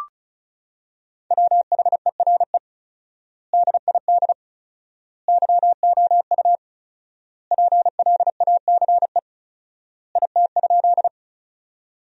Morse code, heard as a keyed tone, about 34 wpm, 700 Hz. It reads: WHERE DID YOU PLACE IT?
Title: 4x Word-Spacing - Sentences from Top 200 Words - 35wpm